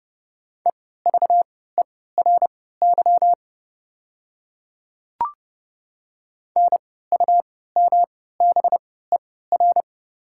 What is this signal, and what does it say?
Morse code, heard as a keyed tone, about 30 wpm, spaced as for 17 wpm, 700 Hz. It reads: EVERY NUMBER
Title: 3x Character Spacing - Top 300 Words - 30wpm